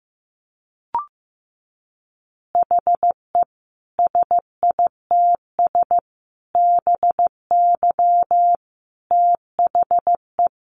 Morse code, heard as a keyed tone, about 15 wpm, 700 Hz. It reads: HE SITS BY THE
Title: Easy Phrases 15wpm